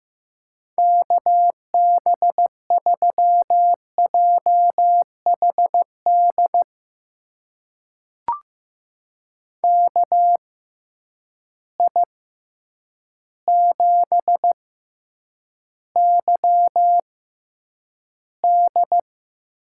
Morse code, heard as a keyed tone, about 16 wpm, spaced as for 8 wpm, 700 Hz. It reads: KB3JHD KI7YD
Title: USA Call Signs - 6 Characters - 6x Character Spacing - 15wpm